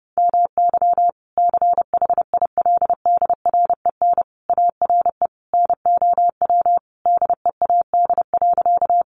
Morse code, heard as a keyed tone, about 30 wpm, 700 Hz. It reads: MY CHILDREN ARE NOW DEAD.
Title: Sentences from Top 1500 Words 30wpm